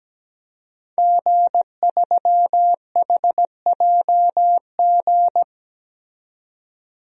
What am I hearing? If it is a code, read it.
Morse code, heard as a keyed tone, about 17 wpm, 700 Hz. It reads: G3HJG
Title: DXCC-100 Call Signs - 5 Characters - 17wpm